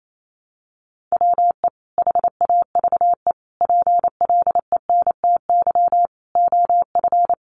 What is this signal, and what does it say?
Morse code, heard as a keyed tone, about 28 wpm, 700 Hz. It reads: WE HAVE PLENTY OF
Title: Sentences from Top 1300 Words 28wpm